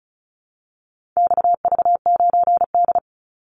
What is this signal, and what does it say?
Morse code, heard as a keyed tone, about 34 wpm, 700 Hz. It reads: XV9D
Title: Call Signs Repeated 3x - 35wpm